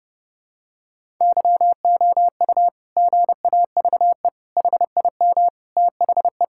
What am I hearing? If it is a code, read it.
Morse code, heard as a keyed tone, about 30 wpm, 700 Hz. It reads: YOU GAVE HIM THE
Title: Sentences from Top 1000 Words - Review Part 6 - 30wpm